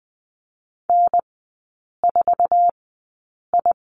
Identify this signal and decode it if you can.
Morse code, heard as a keyed tone, about 20 wpm, spaced as for 8 wpm, 700 Hz. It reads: N4I
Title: 047 Lesson 9e - Number 4 - Three Characters 20wpm